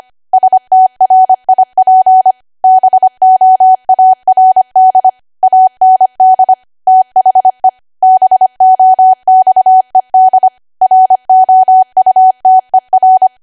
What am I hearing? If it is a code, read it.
Morse code, heard as a keyed tone, about 25 wpm, 750 Hz. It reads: STRIP BOARD AND THE BOXED ROUTER